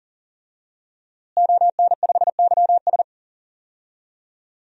Morse code, heard as a keyed tone, about 40 wpm, 700 Hz. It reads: ON5YS
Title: DXCC-100 Call Signs - 5 Characters - Repeated 3x - 40wpm